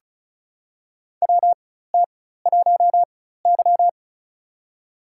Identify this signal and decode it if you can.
Morse code, heard as a keyed tone, about 36 wpm, spaced as for 16 wpm, 700 Hz. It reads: WT1Y
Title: USA Call Signs - 4x Character Spacing - 35wpm